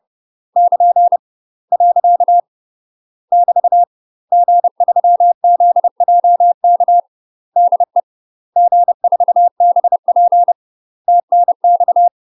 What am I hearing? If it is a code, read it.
Morse code, heard as a keyed tone, about 30 wpm, 700 Hz. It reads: <KN> . <BT> G3ZJK DE G4BP TNX
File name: UK MASTER QSOs TRAINING/30WPM